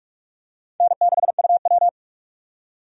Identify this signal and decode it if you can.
Morse code, heard as a keyed tone, about 41 wpm, 700 Hz. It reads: N6UW
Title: USA Call Signs - 4 Characters - 45wpm